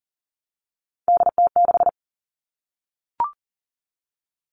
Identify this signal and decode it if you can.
Morse code, heard as a keyed tone, about 40 wpm, 700 Hz. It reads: DT6
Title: All Cut Numbers - 3-Digit 40wpm